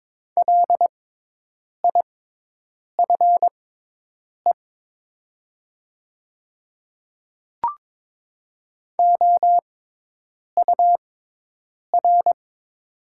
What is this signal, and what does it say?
Morse code, heard as a keyed tone, about 22 wpm, spaced as for 8 wpm, 700 Hz. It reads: LIFE OUR
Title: 6x Character Spacing - Top 300 Words - 22wpm